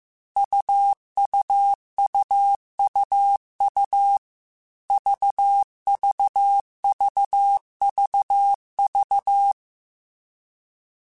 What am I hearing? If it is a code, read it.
Morse code, heard as a keyed tone, about 15 wpm, 800 Hz. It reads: UUUUU VVVVV
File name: Short Runs / Short Runs-mp3s / AUV AUV4 @13